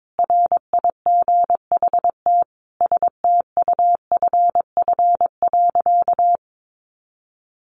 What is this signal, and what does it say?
Morse code, heard as a keyed tone, about 22 wpm, 700 Hz. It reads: RIGHT STUFF.
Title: Sentences from Top 1400 Words 22wpm